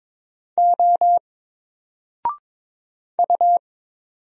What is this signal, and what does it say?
Morse code, heard as a keyed tone, about 22 wpm, 700 Hz. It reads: O U
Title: Single Letter-Number Alt 22wpm